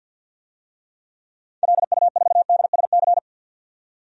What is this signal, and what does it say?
Morse code, heard as a keyed tone, about 51 wpm, 700 Hz. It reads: LU4DSC
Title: DXCC-100 Call Signs - 6 Characters - 50wpm